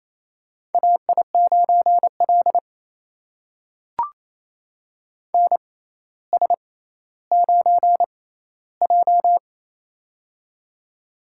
Morse code, heard as a keyed tone, about 28 wpm, spaced as for 14 wpm, 700 Hz. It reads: AI9L NS9J
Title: USA Call Signs - 4 Characters - 6x Character Spacing - 28wpm